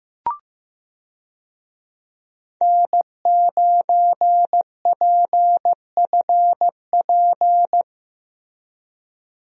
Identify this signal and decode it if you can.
Morse code, heard as a keyed tone, about 15 wpm, 700 Hz. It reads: N9PFP
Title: USA Call Signs - 5 Characters - Repeated 2x - 15wpm